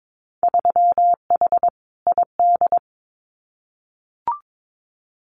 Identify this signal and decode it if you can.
Morse code, heard as a keyed tone, about 22 wpm, 700 Hz. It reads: 3H ID
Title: Winter Field Day Exchanges Encore 22wpm